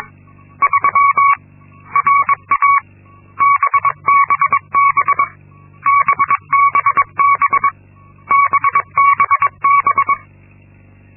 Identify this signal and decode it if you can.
Morse code, heard as a keyed tone, about 22 wpm, 1120 Hz. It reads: 3 RA BBB BBB BBB
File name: Linksmakalnis-2